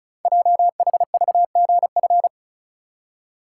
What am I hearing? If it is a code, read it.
Morse code, heard as a keyed tone, about 36 wpm, 700 Hz. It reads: JHVGF
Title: Code Groups - 5 Letters - 35wpm